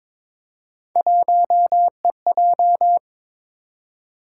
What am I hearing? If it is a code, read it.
Morse code, heard as a keyed tone, about 22 wpm, 700 Hz. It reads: 1EJ
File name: practice / 22wpm / Three Letter-Number Combinations - 22wpm